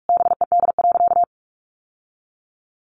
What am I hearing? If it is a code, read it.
Morse code, heard as a keyed tone, about 44 wpm, 700 Hz. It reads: BED.
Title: Sentences from Top 700 Words 45wpm